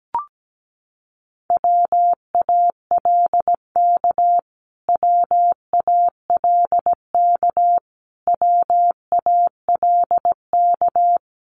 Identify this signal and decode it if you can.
Morse code, heard as a keyed tone, about 17 wpm, 700 Hz. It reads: WALK WALK WALK
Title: Top 500 Words Repeated 17wpm